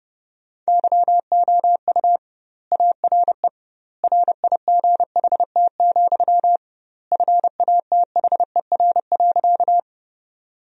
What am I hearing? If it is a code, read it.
Morse code, heard as a keyed tone, about 30 wpm, 700 Hz. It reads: YOU ARE RIGHT, FATHER.
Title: 2x Word-Spacing - Sentences from Top 300 Words - 30wpm